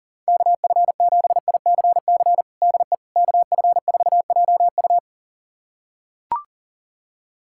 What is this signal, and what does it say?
Morse code, heard as a keyed tone, about 40 wpm, 700 Hz. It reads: KF7ICC DE KF4JU
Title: USA Callsign DE Callsign 40wpm